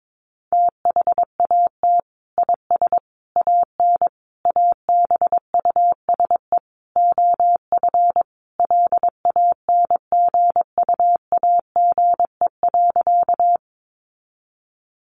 Morse code, heard as a keyed tone, about 22 wpm, 700 Hz. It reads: THAT IS AN ABUSE OF LANGUAGE.
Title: Sentences from Top 1300 Words 22wpm